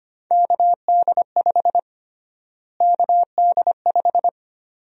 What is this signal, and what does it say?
Morse code, heard as a keyed tone, about 25 wpm, 700 Hz. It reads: KD5 KD5
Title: USA Call Sign Prefixes - Repeated 2x - 25wpm